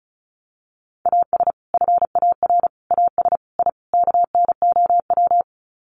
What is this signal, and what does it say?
Morse code, heard as a keyed tone, about 34 wpm, 700 Hz. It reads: AS FAR AS I KNOW
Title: Sets of 5 Words 35wpm